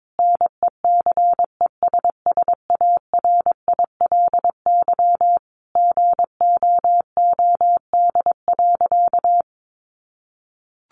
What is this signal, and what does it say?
Morse code, heard as a keyed tone, about 22 wpm, 700 Hz. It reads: NECESSARILY GOOD.
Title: Sentences from Top 1400 Words 22wpm